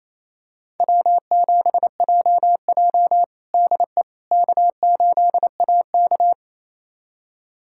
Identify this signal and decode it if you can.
Morse code, heard as a keyed tone, about 28 wpm, 700 Hz. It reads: W7JJ DE K8AK
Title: USA Callsign DE Callsign - 4 Characters - 28wpm